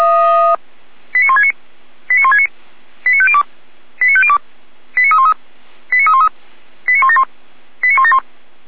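Five-tone selective call (ZVEI), heard as quick runs of five tones, tones of about 70 ms. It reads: 89169 89169 89502 89502 89313 89313 89171 89171
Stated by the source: ZVEI five-tone call received on SDR by feuerwehr-orlamuende.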